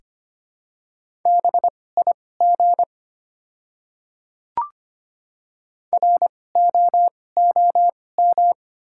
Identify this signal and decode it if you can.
Morse code, heard as a keyed tone, about 25 wpm, spaced as for 18 wpm, 690 Hz. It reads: BIG ROOM
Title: 2x Character Spacing - Top 300 Words - 25wpm